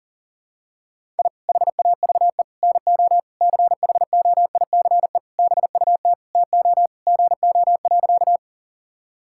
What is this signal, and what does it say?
Morse code, heard as a keyed tone, about 40 wpm, 700 Hz. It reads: I HAVE NO CHOICE BUT TO GO.
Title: Sentences from Top 1000 Words - Review Part 6 - 40wpm